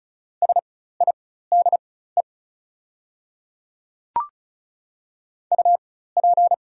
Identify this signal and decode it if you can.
Morse code, heard as a keyed tone, about 36 wpm, spaced as for 16 wpm, 690 Hz. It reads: SIDE UP
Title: 4x Character Spacing - Top 300 Words - 35wpm